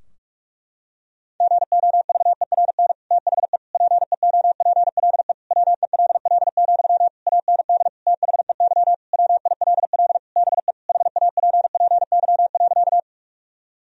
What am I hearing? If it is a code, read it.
Morse code, heard as a keyed tone, about 46 wpm, 700 Hz. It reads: GOVERN THE PEOPLE WELL, AND THEY WILL BE HAPPY.
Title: Sentences from Top 1000 Words - Review Part 2 - 45wpm